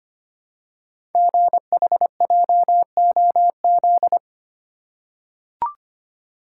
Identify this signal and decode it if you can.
Morse code, heard as a keyed tone, about 25 wpm, 700 Hz. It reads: GHJOZ
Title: Code Groups - 5 Letters - 25wpm